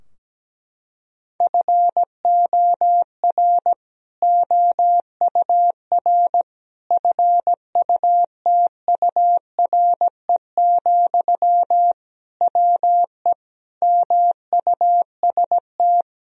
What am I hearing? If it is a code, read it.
Morse code, heard as a keyed tone, about 17 wpm, 700 Hz. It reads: FOR OUR FUTURE, WE MUST
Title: Sentences from Top 1000 Words - Review Part 3 - 17wpm